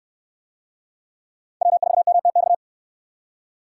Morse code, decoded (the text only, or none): W4MTX